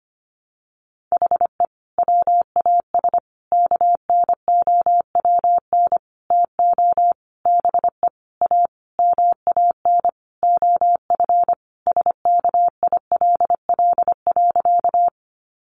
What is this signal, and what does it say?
Morse code, heard as a keyed tone, about 25 wpm, 700 Hz. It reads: HE WAS KNOWN TO BE A MAN OF SKILL.
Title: Sentences from Top 1200 Words 25wpm